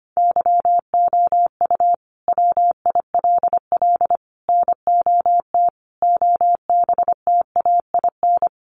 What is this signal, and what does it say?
Morse code, heard as a keyed tone, about 25 wpm, 700 Hz. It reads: YOU WILL NOT OBTAIN
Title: Sentences from Top 1300 Words 25wpm